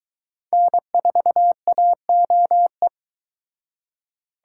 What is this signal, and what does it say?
Morse code, heard as a keyed tone, about 21 wpm, 700 Hz. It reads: N4AOE